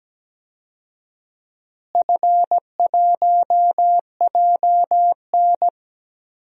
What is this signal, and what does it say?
Morse code, heard as a keyed tone, about 17 wpm, 700 Hz. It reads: F1JN
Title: DXCC-100 Call Signs - 4 Characters - Repeated 3x - 17wpm